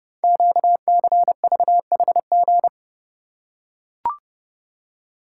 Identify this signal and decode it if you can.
Morse code, heard as a keyed tone, about 30 wpm, 700 Hz. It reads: QCVHG
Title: Code Groups - 5 Letters - 30wpm